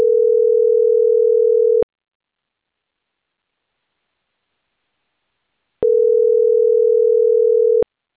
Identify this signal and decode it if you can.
Ringback tone (North America), 440 + 480 Hz.